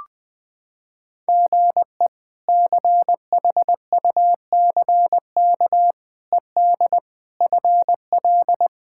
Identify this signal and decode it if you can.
Morse code, heard as a keyed tone, about 20 wpm, 700 Hz. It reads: GE CHUCK ED FL
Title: K1USN SST QSO Practice - 20wpm